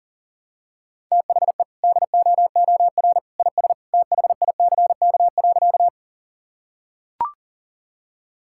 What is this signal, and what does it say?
Morse code, heard as a keyed tone, about 40 wpm, 700 Hz. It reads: THE DOOR IS THICK.
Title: Sentences from Top 1600 Words 40wpm